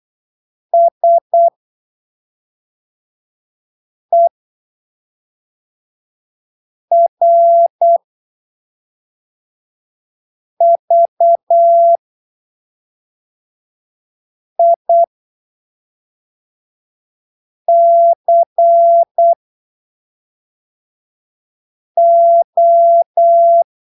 Morse code, heard as a keyed tone, about 8 wpm, spaced as for 3 wpm, 680 Hz. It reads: SERVICO
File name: arquivos / texto / texto 01 08ppm